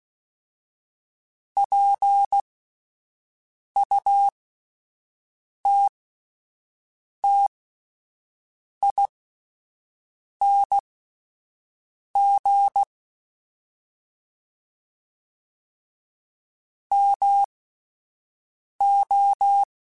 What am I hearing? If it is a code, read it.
Morse code, heard as a keyed tone, about 16 wpm, spaced as for 5 wpm, 800 Hz. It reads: PUTTING MO